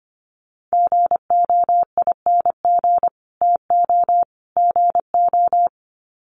Morse code, heard as a keyed tone, about 25 wpm, 700 Hz. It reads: GOING TO GO
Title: Sets of 3 Words 25wpm